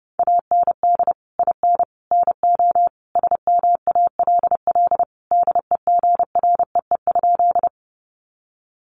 Morse code, heard as a keyed tone, about 30 wpm, 700 Hz. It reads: AND IN NO SMALL DEGREE?
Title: Sentences from Top 1500 Words 30wpm